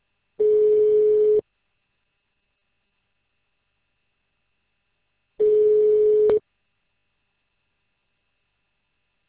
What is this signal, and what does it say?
Ringback tone (Europe, 425 Hz).